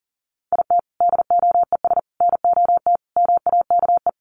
Morse code, heard as a keyed tone, about 40 wpm, 700 Hz. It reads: IT DOES NOT MAKE
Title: Sets of 5 Words Encore2 40wpm